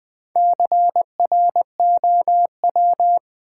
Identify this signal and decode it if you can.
Morse code, heard as a keyed tone, about 20 wpm, 700 Hz. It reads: CROW
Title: Build A Word - 1 Letter At A Time - 5-Letter Words - 20wpm